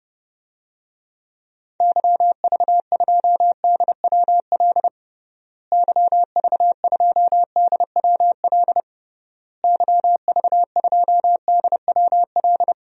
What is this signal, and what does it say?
Morse code, heard as a keyed tone, about 30 wpm, 700 Hz. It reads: YV2DWL YV2DWL YV2DWL
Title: DXCC-100 Call Signs - 6 Characters - Repeated 3x - 30wpm